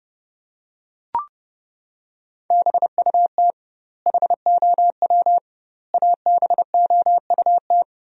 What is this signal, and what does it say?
Morse code, heard as a keyed tone, about 30 wpm, 700 Hz. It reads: BUT HOW ABOUT
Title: 2x Word-Spacing - Sentences from Top 100 Words - 30wpm